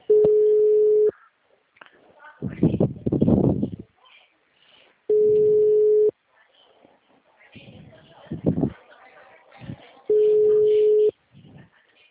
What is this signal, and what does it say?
Ringback tone (Europe, 425 Hz), with speech.